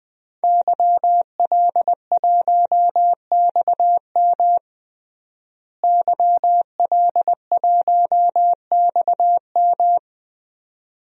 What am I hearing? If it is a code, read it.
Morse code, heard as a keyed tone, about 20 wpm, 700 Hz. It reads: YL1XM YL1XM
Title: DXCC-100 Call Signs - 5 Characters - Repeated 2x - 20wpm